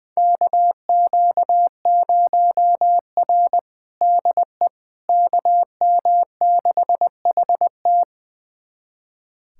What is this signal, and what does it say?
Morse code, heard as a keyed tone, about 20 wpm, 700 Hz. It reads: KQ0R DE KM6HT
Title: USA Callsign DE Callsign 20wpm